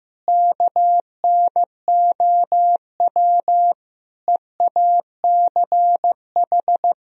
Morse code, heard as a keyed tone, about 15 wpm, 700 Hz. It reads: KNOW EACH
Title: Sentences from Top 1000 Words - Review Part 3 - 15wpm